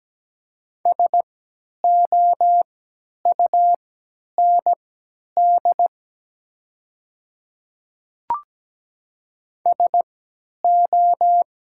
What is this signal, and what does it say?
Morse code, heard as a keyed tone, about 17 wpm, spaced as for 10 wpm, 700 Hz. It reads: SOUND SO
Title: 3x Character Spacing - Top 100 Words - 17wpm